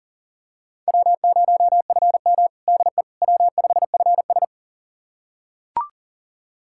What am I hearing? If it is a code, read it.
Morse code, heard as a keyed tone, about 37 wpm, 700 Hz. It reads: W0FM DE W5FS
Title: USA Callsign DE Callsign - 4 Characters - 40wpm